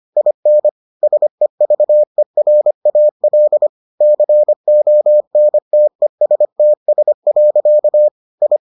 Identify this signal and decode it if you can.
Morse code, heard as a keyed tone, about 25 wpm, 600 Hz. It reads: IN SEVERAL CONTESTS. I